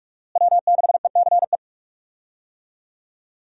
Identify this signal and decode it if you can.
Morse code, heard as a keyed tone, about 46 wpm, 700 Hz. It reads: W6ECE
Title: USA Call Signs - 5 Characters - Repeated 2x - 45wpm